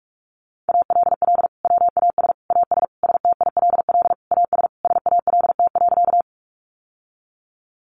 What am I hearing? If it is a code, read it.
Morse code, heard as a keyed tone, about 44 wpm, 700 Hz. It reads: ALL WAS AS STILL AS SALT.
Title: Sentences from Top 800 Words 45wpm